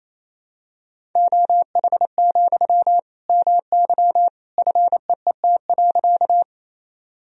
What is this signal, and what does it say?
Morse code, heard as a keyed tone, about 28 wpm, 700 Hz. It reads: OH, MY FEET.
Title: Sentences from Top 1000 Words - Review Part 1 - 28wpm